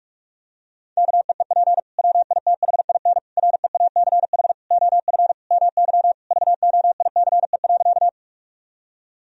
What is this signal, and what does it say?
Morse code, heard as a keyed tone, about 46 wpm, 700 Hz. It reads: KEEP WITHIN REACH OF MY VOICE.
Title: Sentences from Top 1000 Words - Review Part 4 - 45wpm